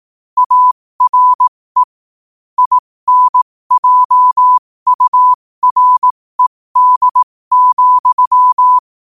Morse code, heard as a keyed tone, about 18 wpm, spaced as for 15 wpm, 1000 Hz. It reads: ARE INJURED,
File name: qotdfiles / 15wpm